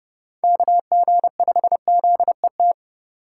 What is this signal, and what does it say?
Morse code, heard as a keyed tone, about 30 wpm, 700 Hz. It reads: KG5ZET